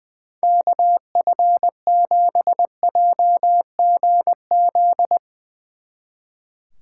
Morse code, heard as a keyed tone, about 20 wpm, 700 Hz. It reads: KF7JGZ
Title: Speed Racing - USA 2x3 Call Signs 20wpm